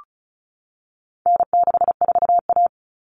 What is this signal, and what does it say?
Morse code, heard as a keyed tone, about 34 wpm, 700 Hz. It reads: N64A
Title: Common Cut Numbers - 4-Digit 35wpm